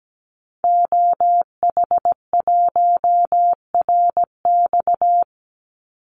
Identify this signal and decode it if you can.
Morse code, heard as a keyed tone, about 17 wpm, 700 Hz. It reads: OH1RX
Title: Call Signs - 17wpm